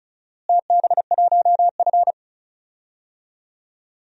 Morse code, heard as a keyed tone, about 34 wpm, 700 Hz. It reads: TB1F
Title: DXCC-100 Call Signs - 4 Characters - Repeated 2x - 35wpm